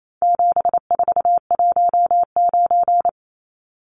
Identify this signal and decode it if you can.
Morse code, heard as a keyed tone, about 28 wpm, 700 Hz. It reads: 7419
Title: Numbers - 4-Digit 28wpm